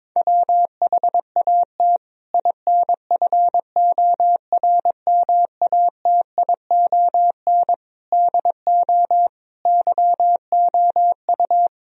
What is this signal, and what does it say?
Morse code, heard as a keyed tone, about 22 wpm, 700 Hz. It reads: WHAT INFORMATION DO YOU
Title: Sentences from Top 1000 Words - Review Part 3 - 22wpm